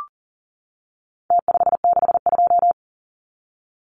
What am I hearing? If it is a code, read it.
Morse code, heard as a keyed tone, about 40 wpm, 700 Hz. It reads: T562
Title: Numbers - Contest Style - 4-Digit 40wpm